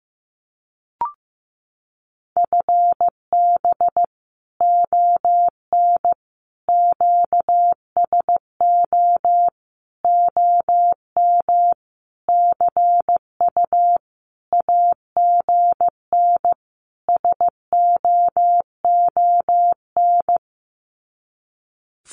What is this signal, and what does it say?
Morse code, heard as a keyed tone, about 15 wpm, 700 Hz. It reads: FB ON QSO OM CU AGN SOON
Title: Common QSO Phrases 15wpm